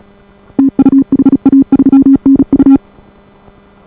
Morse code, heard as a keyed tone, about 37 wpm, 280 Hz. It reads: TU HA3NU